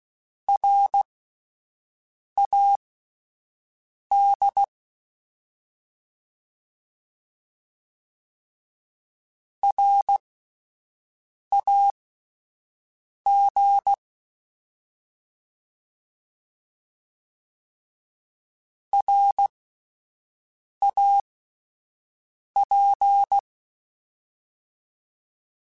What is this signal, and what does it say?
Morse code, heard as a keyed tone, about 16 wpm, spaced as for 5 wpm, 800 Hz. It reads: RAD RAG RAP